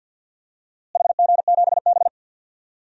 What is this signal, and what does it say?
Morse code, heard as a keyed tone, about 49 wpm, 700 Hz. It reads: HG7B
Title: DXCC-100 Call Signs - 4 Characters - Repeated 2x - 50wpm